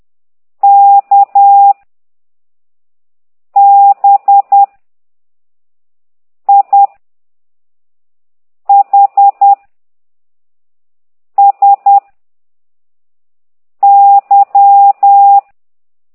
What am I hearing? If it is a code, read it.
Morse code, heard as a keyed tone, about 10 wpm, 800 Hz. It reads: K B I H S Y
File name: exerc7